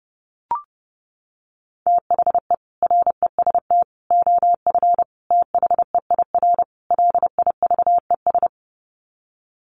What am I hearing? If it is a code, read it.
Morse code, heard as a keyed tone, about 28 wpm, 700 Hz. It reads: THE REST OF THEIR LIVES
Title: Sets of 5 Words Encore 30wpm